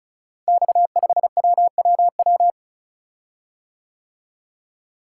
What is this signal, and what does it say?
Morse code, heard as a keyed tone, about 36 wpm, 700 Hz. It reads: X5WWW
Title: Code Groups - 5 Letter:Numbers - 35wpm